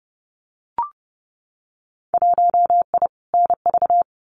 Morse code, heard as a keyed tone, about 30 wpm, 700 Hz. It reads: 1I NV
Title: Winter Field Day Exchanges Encore 30wpm